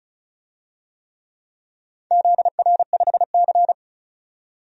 Morse code, heard as a keyed tone, about 36 wpm, 700 Hz. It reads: ZR5C
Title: DXCC-100 Call Signs - 4 Characters - 35wpm